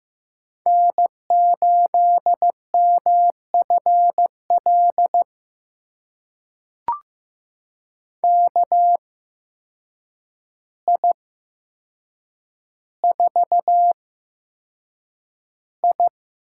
Morse code, heard as a keyed tone, about 15 wpm, spaced as for 7 wpm, 700 Hz. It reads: N8MFL KI4I
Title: USA Call Signs - 5 Characters - 8x Character Spacing - 15wpm